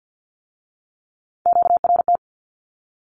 Morse code, heard as a keyed tone, about 50 wpm, 700 Hz. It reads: QRT
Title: Q-Codes with Spoken Meaning - 50wpm